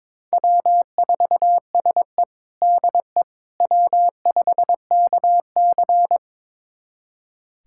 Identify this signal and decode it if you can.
Morse code, heard as a keyed tone, about 22 wpm, 700 Hz. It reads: W4SE DE W5KC
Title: USA Callsign DE Callsign - 4 Characters - 22wpm